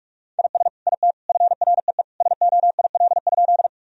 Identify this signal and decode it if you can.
Morse code, heard as a keyed tone, about 43 wpm, 700 Hz. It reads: IS IT FREE SOIL?
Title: Sentences from Top 1000 Words - Review Part 4 - 45wpm